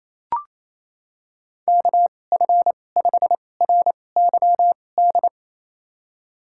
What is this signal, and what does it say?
Morse code, heard as a keyed tone, about 28 wpm, spaced as for 20 wpm, 700 Hz. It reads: KF5RYD
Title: USA Call Signs - 6 Characters - 2x Character Spacing - 28wpm